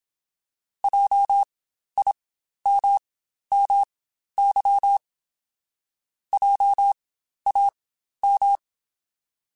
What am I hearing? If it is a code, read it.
Morse code, heard as a keyed tone, about 27 wpm, spaced as for 12 wpm, 800 Hz. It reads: JIMMY JAM
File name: Spaced Alphas / Spaced Alphas mp3s / Js sp @25